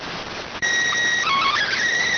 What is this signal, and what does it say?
SSTV (slow-scan television), PD 290.